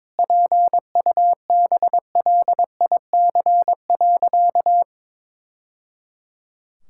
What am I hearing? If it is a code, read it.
Morse code, heard as a keyed tone, about 22 wpm, 700 Hz. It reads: PUBLIC.
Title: Sentences from Top 1000 Words - Review Part 5 - 22wpm